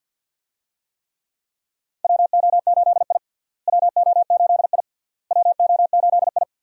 Morse code, heard as a keyed tone, about 48 wpm, 700 Hz. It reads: WO8I WO8I WO8I
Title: USA Call Signs - 4 Characters - Repeated 3x - 50wpm